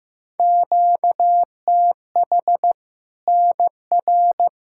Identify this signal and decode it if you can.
Morse code, heard as a keyed tone, about 16 wpm, 700 Hz. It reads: QTH NR
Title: K1USN SST QSO Practice 15wpm